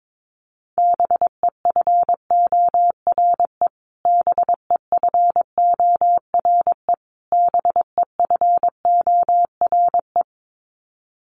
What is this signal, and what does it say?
Morse code, heard as a keyed tone, about 22 wpm, 700 Hz. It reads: BEFORE BEFORE BEFORE
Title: Top 200 Words Repeated 22wpm